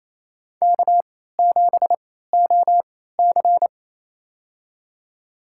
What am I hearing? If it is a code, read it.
Morse code, heard as a keyed tone, about 29 wpm, spaced as for 16 wpm, 700 Hz. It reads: K7OC
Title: USA Call Signs - 4 Characters - 3x Character Spacing - 28wpm